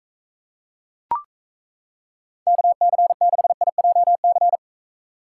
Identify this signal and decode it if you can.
Morse code, heard as a keyed tone, about 36 wpm, 700 Hz. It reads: KC6IJC